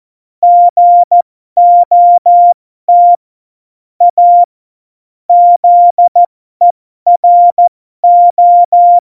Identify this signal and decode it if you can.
Morse code, heard as a keyed tone, about 14 wpm, spaced as for 12 wpm, 700 Hz. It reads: GOT A ZERO
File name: Morsekurs / Witze